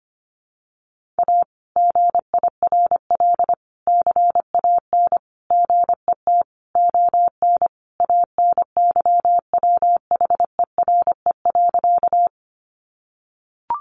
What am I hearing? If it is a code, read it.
Morse code, heard as a keyed tone, about 25 wpm, 700 Hz. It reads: A GIRL CAN GET ON ANYWHERE.
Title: Sentences from Top 1400 Words 25wpm